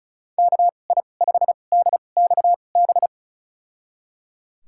Morse code, heard as a keyed tone, about 34 wpm, spaced as for 25 wpm, 690 Hz. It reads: KI5DXB
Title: USA Call Signs - 6 Characters - 2x Character Spacing - 35wpm